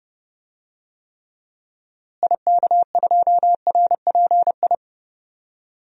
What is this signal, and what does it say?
Morse code, heard as a keyed tone, about 31 wpm, 700 Hz. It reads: IK2RPI
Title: DXCC-100 Call Signs - 6 Characters - Repeated 2x - 30wpm